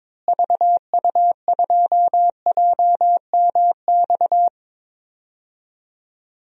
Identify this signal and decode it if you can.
Morse code, heard as a keyed tone, about 22 wpm, 700 Hz. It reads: VU2JMX
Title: DXCC-100 Call Signs - 6 Characters - Repeated 3x - 22wpm